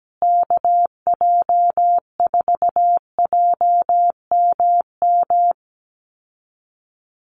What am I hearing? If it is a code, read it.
Morse code, heard as a keyed tone, about 17 wpm, 700 Hz. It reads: KJ4JMM
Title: USA Call Signs Repeated 3x 17wpm